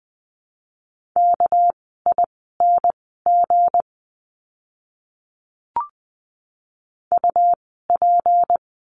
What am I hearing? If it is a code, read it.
Morse code, heard as a keyed tone, about 20 wpm, spaced as for 15 wpm, 700 Hz. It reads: KING UP
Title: 199 Lesson 32f - Letter k - Words 20wpm